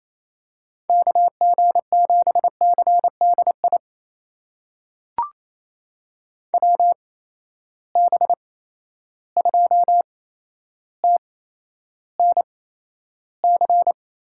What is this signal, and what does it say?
Morse code, heard as a keyed tone, about 28 wpm, spaced as for 12 wpm, 700 Hz. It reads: KG7CDI WB2TNC
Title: USA Call Signs - 6 Characters - 8x Character Spacing - 28wpm